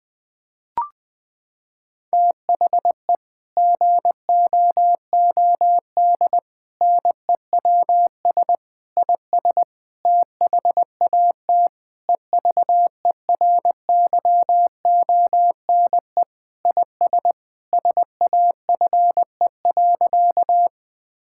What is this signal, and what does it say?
Morse code, heard as a keyed tone, about 21 wpm, 700 Hz. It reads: THE GOOD NEWS IS THAT EVERYONE IS SAFE.
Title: Sentences with 5-Word Phrases - 20wpm